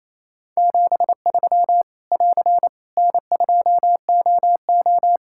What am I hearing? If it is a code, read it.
Morse code, heard as a keyed tone, about 28 wpm, 700 Hz. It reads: 73 <AR> N2OO
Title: Practice QSO - Exchange 4 - Ending - 28wpm